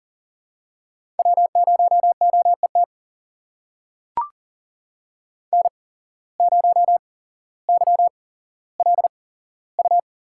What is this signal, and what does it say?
Morse code, heard as a keyed tone, about 39 wpm, spaced as for 17 wpm, 700 Hz. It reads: W0OET N0YLU
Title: USA Call Signs - 5 Characters - 8x Character Spacing - 40wpm